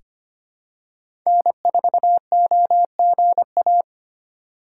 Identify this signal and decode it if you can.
Morse code, heard as a keyed tone, about 25 wpm, 700 Hz. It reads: N4OGA
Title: USA Call Signs - 5 Characters - 3x Character Spacing - 25wpm